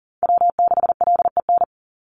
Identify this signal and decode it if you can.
Morse code, heard as a keyed tone, about 41 wpm, 700 Hz. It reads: W6LEN
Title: Call Signs - 40wpm